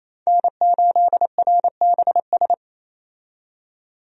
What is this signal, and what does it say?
Morse code, heard as a keyed tone, about 28 wpm, 700 Hz. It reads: N8RBS
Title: USA Call Signs - 5 Characters - 28wpm